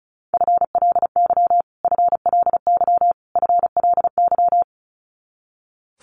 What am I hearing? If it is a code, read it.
Morse code, heard as a keyed tone, about 34 wpm, 700 Hz. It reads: FLY FLY FLY
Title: Top 500 Words Repeated 35wpm